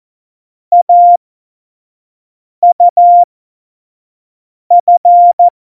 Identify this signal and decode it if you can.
Morse code, heard as a keyed tone, about 14 wpm, spaced as for 5 wpm, 700 Hz. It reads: AUF
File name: Morsekurs / Witze / Witz 08. Januar 2019, 25 F70